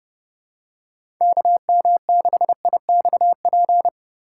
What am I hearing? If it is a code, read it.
Morse code, heard as a keyed tone, about 30 wpm, 700 Hz. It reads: KM6IXP
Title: USA Call Signs - 4x Character Spacing - 30wpm